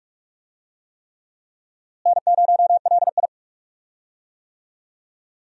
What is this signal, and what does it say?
Morse code, heard as a keyed tone, about 42 wpm, 700 Hz. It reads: N0LI
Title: USA Call Signs - 4 Characters - Repeated 3x - 45wpm